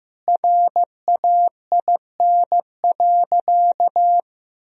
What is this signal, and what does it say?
Morse code, heard as a keyed tone, about 15 wpm, 700 Hz. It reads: RAIN.
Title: Sentences from Top 1000 Words - Review Part 1 - 15wpm